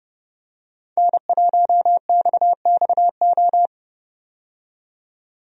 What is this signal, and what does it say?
Morse code, heard as a keyed tone, about 30 wpm, 700 Hz. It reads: N1XXO
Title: USA Call Signs - 5 Characters - 30wpm